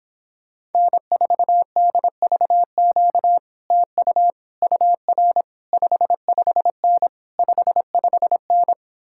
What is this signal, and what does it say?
Morse code, heard as a keyed tone, about 29 wpm, 700 Hz. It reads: N4DVQ TU UR 55N 55N